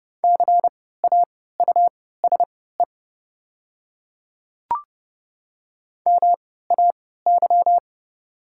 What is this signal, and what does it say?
Morse code, heard as a keyed tone, about 31 wpm, spaced as for 17 wpm, 700 Hz. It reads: CAUSE MAY
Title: 3x Character Spacing - Top 200 Words - 30wpm